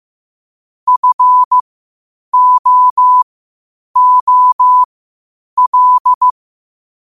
Morse code, heard as a keyed tone, about 15 wpm, spaced as for 8 wpm, 1000 Hz. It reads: FOOL
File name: qotdfiles / 8wpm